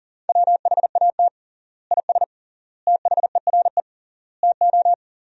Morse code, heard as a keyed tone, about 40 wpm, 700 Hz. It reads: WHAT IS THERE TO
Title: 3x Word-Spacing - Sentences from Top 400 Words - 40wpm